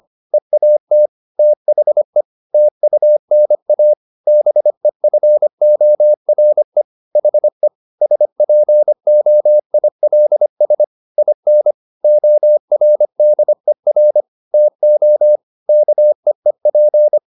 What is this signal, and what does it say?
Morse code, heard as a keyed tone, about 25 wpm, 600 Hz. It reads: EAT THE TUNA BEFORE HE SPOILS IN ORDER TO KEEP
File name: the old man and the sea part 1 25wpm